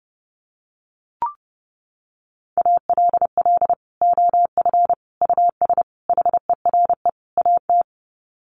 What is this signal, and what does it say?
Morse code, heard as a keyed tone, about 30 wpm, 700 Hz. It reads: ALL OF US HERE AT
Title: Sets of 5 Words 30wpm